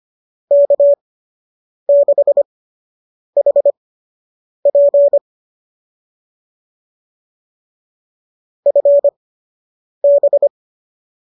Morse code, heard as a keyed tone, about 25 wpm, spaced as for 8 wpm, 590 Hz. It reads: K6HP FB
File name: qso003 08